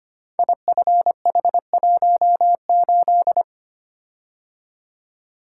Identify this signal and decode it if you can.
Morse code, heard as a keyed tone, about 25 wpm, 700 Hz. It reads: IFH18
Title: Code Groups - 5 Letter:Numbers - 25wpm